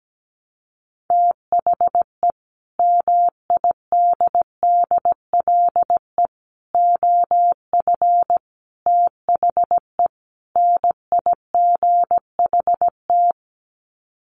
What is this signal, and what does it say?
Morse code, heard as a keyed tone, about 17 wpm, 700 Hz. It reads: THE MIDDLE OF THE NIGHT
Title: Sets of 5 Words 17wpm